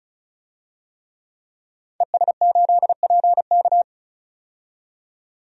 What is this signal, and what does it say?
Morse code, heard as a keyed tone, about 36 wpm, 700 Hz. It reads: ES8PK
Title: DXCC-100 Call Signs - 5 Characters - 35wpm